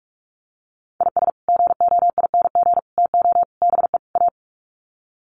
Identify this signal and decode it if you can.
Morse code, heard as a keyed tone, about 44 wpm, 700 Hz. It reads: IS GOING TO BE A
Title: Sets of 5 Words 45wpm